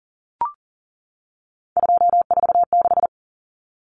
Morse code, heard as a keyed tone, about 40 wpm, 700 Hz. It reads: 246
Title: Numbers - 3-Digit 40wpm